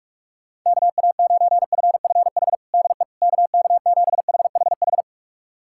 Morse code, heard as a keyed tone, about 46 wpm, 700 Hz. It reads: KA9FFH DE KK7HHH
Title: USA Callsign DE Callsign - 6 Characters - 45wpm